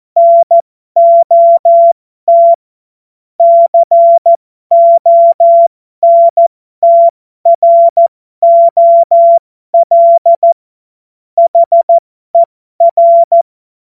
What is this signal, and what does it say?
Morse code, heard as a keyed tone, about 14 wpm, spaced as for 12 wpm, 680 Hz. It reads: NOT CONTROL HER
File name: Morsekurs / Witze